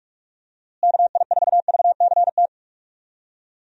Morse code, heard as a keyed tone, about 46 wpm, 700 Hz. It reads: KI4VCT
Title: USA Call Signs - 6 Characters - Repeated 3x - 45wpm